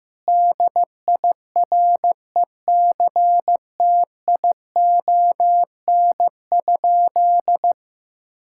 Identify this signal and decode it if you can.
Morse code, heard as a keyed tone, about 15 wpm, 700 Hz. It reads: DIRECTION?
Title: Sentences from Top 1000 Words - Review Part 5 - 15wpm